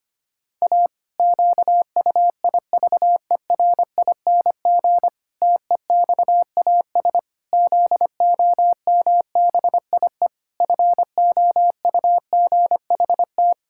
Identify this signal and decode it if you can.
Morse code, heard as a keyed tone, about 25 wpm, 700 Hz. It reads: A QUIVERING TEXAS ZOMBIE FOUGHT
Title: Pangram Challenge v2 - 25wpm